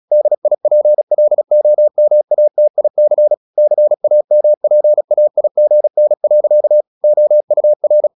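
Morse code, heard as a keyed tone, about 36 wpm, 600 Hz. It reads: DIPLOMATIC CAMPAIGN. OUR
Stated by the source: Morse practice at 36 wpm, text of Ken Follett's novel Never, from CW med Gnister.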